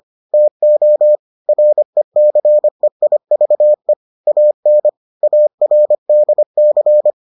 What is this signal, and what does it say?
Morse code, heard as a keyed tone, about 25 wpm, 600 Hz. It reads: TO RECEIVE AN ARDC